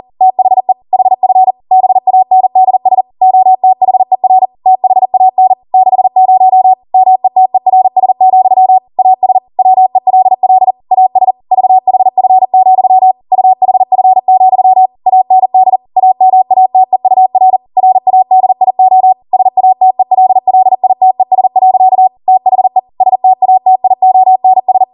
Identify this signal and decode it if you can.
Morse code, heard as a keyed tone, about 40 wpm, 750 Hz. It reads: THE HF BANDS OTHER THAN 60 METERS, AS WELL AS VHF, UHF, AND AMATEUR RADIO SATELLITES. THE STATIONS